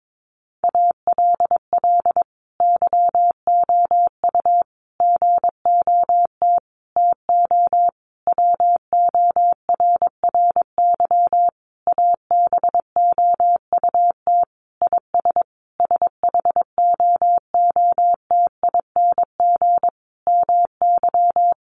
Morse code, heard as a keyed tone, about 22 wpm, 700 Hz. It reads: ALL YOU GOT TO WORRY ABOUT IS SHOOTING MY
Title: A Sound Of Thunder 22wpm